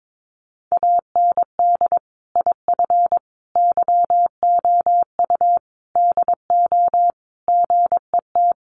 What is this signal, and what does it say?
Morse code, heard as a keyed tone, about 22 wpm, 700 Hz. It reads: AND IF YOU DO GET
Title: Sentences from Top 1500 Words 22wpm